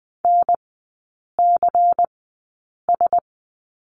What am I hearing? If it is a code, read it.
Morse code, heard as a keyed tone, about 20 wpm, spaced as for 8 wpm, 700 Hz. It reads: NCS
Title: 159 Lesson 26e - Letter g - Three Characters 20wpm